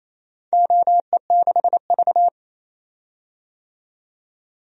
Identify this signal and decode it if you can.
Morse code, heard as a keyed tone, about 28 wpm, 700 Hz. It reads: OE6V
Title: DXCC-100 Call Signs - 4 Characters - 28wpm